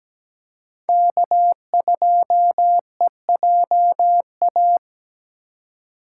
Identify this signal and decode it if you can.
Morse code, heard as a keyed tone, about 17 wpm, 700 Hz. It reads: K2EJA
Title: USA Call Signs - 5 Characters - 2x Character Spacing - 17wpm